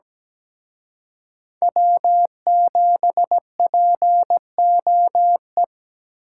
Morse code, heard as a keyed tone, about 20 wpm, 700 Hz. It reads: W7POE
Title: Speed Racing - USA Call Signs 17wpm